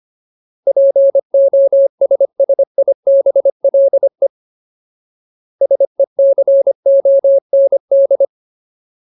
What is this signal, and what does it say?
Morse code, heard as a keyed tone, about 25 wpm, 550 Hz. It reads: POSSIBLE SECOND
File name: Ultimate Word List / 25WPM Ultimate